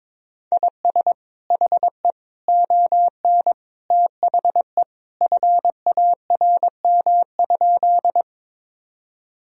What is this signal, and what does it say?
Morse code, heard as a keyed tone, about 22 wpm, 700 Hz. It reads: IS HE ON THE FARM?
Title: Sentences from Top 1000 Words - Review Part 2 - 22wpm